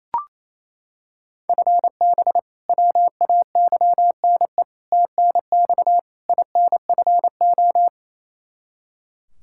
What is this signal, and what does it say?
Morse code, heard as a keyed tone, about 28 wpm, 700 Hz. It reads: FB WAYNE TNX INFO
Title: K1USN SST QSO Practice 28wpm